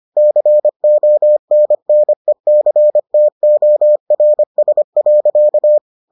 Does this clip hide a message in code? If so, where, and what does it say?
Yes: Morse code from 0.2 to 5.8 s, heard as a keyed tone, about 25 wpm, 600 Hz. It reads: CONNECTORS.